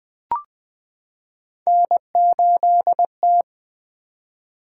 Morse code, heard as a keyed tone, about 20 wpm, 700 Hz. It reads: N8T
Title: Three Letter-Number Combinations - 20wpm